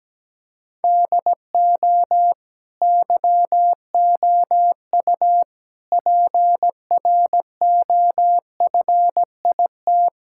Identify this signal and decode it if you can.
Morse code, heard as a keyed tone, about 17 wpm, 700 Hz. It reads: DO YOU PROFIT
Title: Sentences from Top 1000 Words - Review Part 5 - 17wpm